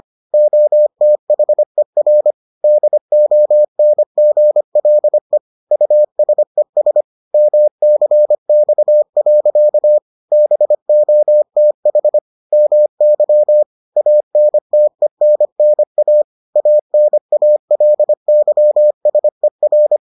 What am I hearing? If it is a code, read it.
Morse code, heard as a keyed tone, about 25 wpm, 600 Hz. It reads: OTHER DONGLE USES MCX. BOTH MY ANTENNA ANALYSER